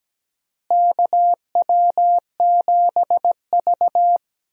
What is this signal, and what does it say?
Morse code, heard as a keyed tone, about 17 wpm, 700 Hz. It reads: KW7V
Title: USA Call Signs - 4 Characters - Repeated 2x - 17wpm